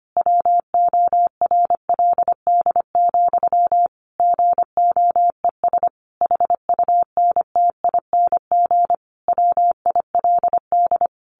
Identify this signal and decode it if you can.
Morse code, heard as a keyed tone, about 25 wpm, 700 Hz. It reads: WORLD, GOES HUNTING WILD
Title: A Sound Of Thunder 25wpm